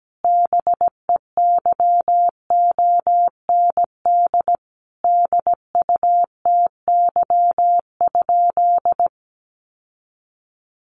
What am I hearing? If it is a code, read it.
Morse code, heard as a keyed tone, about 17 wpm, 700 Hz. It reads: BEYOND DUTY?
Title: Sentences from Top 1500 Words 17wpm